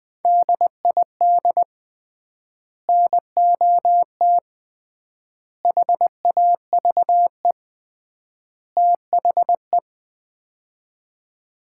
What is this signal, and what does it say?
Morse code, heard as a keyed tone, about 20 wpm, 700 Hz. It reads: DID NOT HAVE THE
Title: 3x Word-Spacing - Sets of 4 Words - 20wpm